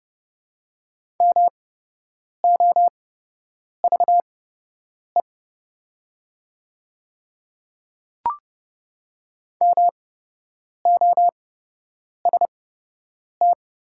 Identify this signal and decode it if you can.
Morse code, heard as a keyed tone, about 30 wpm, spaced as for 8 wpm, 700 Hz. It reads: MOVE MOST
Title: 8x Character Spacing - Top 200 Words - 30wpm